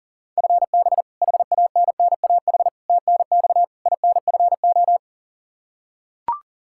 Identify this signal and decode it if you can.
Morse code, heard as a keyed tone, about 40 wpm, 700 Hz. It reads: FB HANNAH TNX INFO
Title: Practice QSO - Exchange 3 - More Info - 40wpm